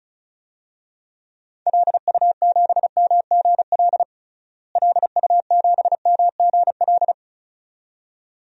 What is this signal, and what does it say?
Morse code, heard as a keyed tone, about 34 wpm, 700 Hz. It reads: LU7MGL LU7MGL
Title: DXCC-100 Call Signs - 6 Characters - Repeated 2x - 35wpm